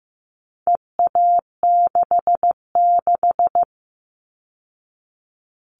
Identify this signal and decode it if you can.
Morse code, heard as a keyed tone, about 15 wpm, 700 Hz. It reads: EA66
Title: All Cut Numbers - 4-Digit 15wpm